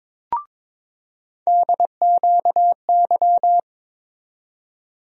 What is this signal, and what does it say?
Morse code, heard as a keyed tone, about 22 wpm, 700 Hz. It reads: DQY
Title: USA Call Sign Postfixes - 3 Letters 22wpm